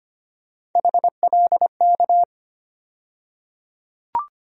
Morse code, heard as a keyed tone, about 25 wpm, 700 Hz. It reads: HLK
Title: USA Call Sign Postfixes - 3 Letters 25wpm